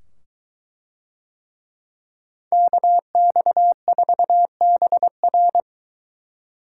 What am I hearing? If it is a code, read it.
Morse code, heard as a keyed tone, about 22 wpm, 700 Hz. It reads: KX4BR